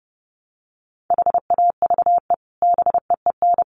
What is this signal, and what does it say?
Morse code, heard as a keyed tone, about 30 wpm, 700 Hz. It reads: HAVE BEEN
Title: Sets of 3 Words 30wpm